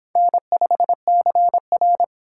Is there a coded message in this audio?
Yes: Morse code from 0.2 to 2.0 s, heard as a keyed tone, about 30 wpm, 700 Hz. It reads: N5CR